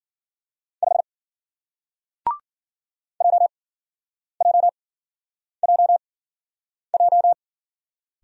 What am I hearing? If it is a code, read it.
Morse code, heard as a keyed tone, about 48 wpm, spaced as for 9 wpm, 700 Hz. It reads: 5 JJJJ
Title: Speed-Racing Single Letter-Number 40wpm